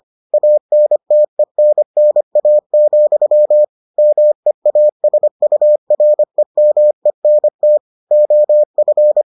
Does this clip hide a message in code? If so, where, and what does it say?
Yes: Morse code from 0.3 to 9.2 s, heard as a keyed tone, about 25 wpm, 600 Hz. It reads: ANTENNA, MEASUREMENT OF